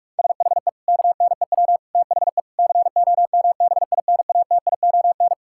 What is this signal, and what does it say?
Morse code, heard as a keyed tone, about 41 wpm, 700 Hz. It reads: SHE KNEW THE COMBINATION
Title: Sentences from Top 1000 Words - Review Part 6 - 45wpm